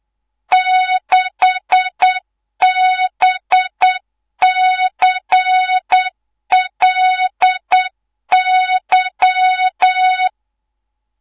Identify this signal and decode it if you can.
Morse code, heard as a keyed tone, about 8 wpm, 3700 Hz, with noise. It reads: 6BCLY
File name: Morze-HA2MN / hangp / 2 03 GYAKORLAT